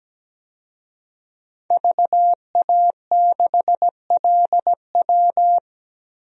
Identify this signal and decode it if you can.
Morse code, heard as a keyed tone, about 17 wpm, 700 Hz. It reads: VA6LW
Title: DXCC-100 Call Signs - 5 Characters - 17wpm